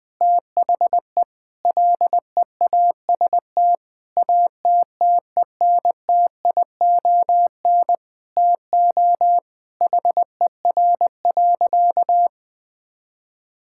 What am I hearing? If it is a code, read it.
Morse code, heard as a keyed tone, about 21 wpm, 700 Hz. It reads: THE LEAST ATTENTION TO HER.
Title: Sentences from Top 1000 Words - Review Part 6 - 20wpm